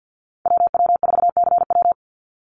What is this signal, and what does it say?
Morse code, heard as a keyed tone, about 50 wpm, 700 Hz. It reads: WW4CW
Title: USA Call Signs 50wpm